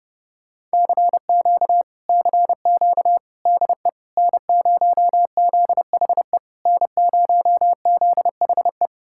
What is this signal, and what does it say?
Morse code, heard as a keyed tone, about 33 wpm, 700 Hz. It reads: CQ CQ DE N0ZHE N0ZHE